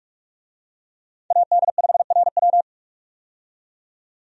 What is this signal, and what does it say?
Morse code, heard as a keyed tone, about 46 wpm, 700 Hz. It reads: AD5RW
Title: USA Call Signs - 5 Characters - 45wpm